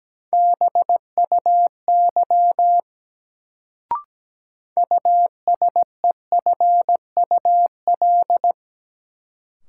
Morse code, heard as a keyed tone, about 17 wpm, 700 Hz. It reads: BUY USEFUL
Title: Sentences from Top 700 Words - New Word Drill - 17wpm